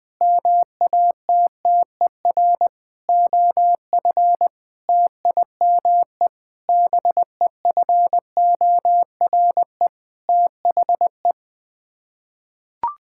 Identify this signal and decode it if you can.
Morse code, heard as a keyed tone, about 20 wpm, 700 Hz. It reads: MATTER OF TIME BEFORE THE
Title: Sets of 9 Words - 20wpm